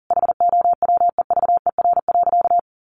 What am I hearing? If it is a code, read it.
Morse code, heard as a keyed tone, about 40 wpm, 700 Hz. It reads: HOWEVER.
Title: Sentences from Top 1200 Words 40wpm